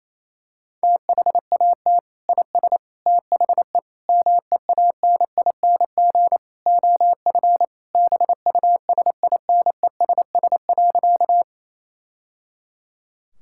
Morse code, heard as a keyed tone, about 28 wpm, 700 Hz. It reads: THAT IS THE MEANING OF BUSINESS.
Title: Sentences from Top 1000 Words - Review Part 6 - 28wpm